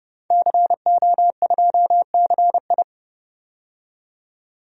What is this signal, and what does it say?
Morse code, heard as a keyed tone, about 30 wpm, 700 Hz. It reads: CO2CI
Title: DXCC-100 Call Signs - 5 Characters - 30wpm